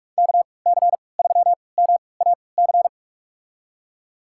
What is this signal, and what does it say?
Morse code, heard as a keyed tone, about 46 wpm, spaced as for 25 wpm, 690 Hz. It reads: KC3MAC
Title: USA Call Signs - 6 Characters - 3x Character Spacing - 45wpm